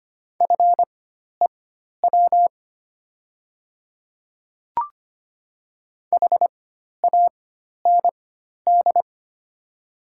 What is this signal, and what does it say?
Morse code, heard as a keyed tone, about 25 wpm, spaced as for 12 wpm, 700 Hz. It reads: FEW HAND
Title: 4x Character Spacing - Top 300 Words - 25wpm